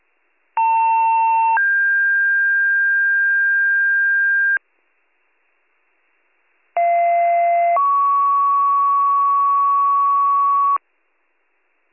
Two-tone page, heard as a one-second tone, then a three-second tone, about 900 then 1600 Hz, about 690 then 1120 Hz. It is heard from 0.6 to 10.8 s.